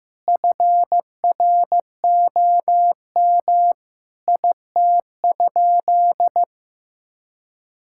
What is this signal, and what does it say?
Morse code, heard as a keyed tone, about 15 wpm, 700 Hz. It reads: FROM IT?
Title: Sentences from Top 1000 Words - Review Part 5 - 15wpm